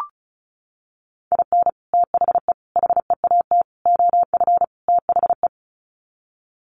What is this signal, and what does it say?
Morse code, heard as a keyed tone, about 34 wpm, 700 Hz. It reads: IN THE HEAT OF THE
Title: Sets of 5 Words Encore2 35wpm